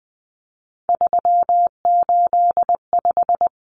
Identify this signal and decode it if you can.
Morse code, heard as a keyed tone, about 20 wpm, 700 Hz. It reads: 385
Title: Common Cut Numbers - 3-Digit 20wpm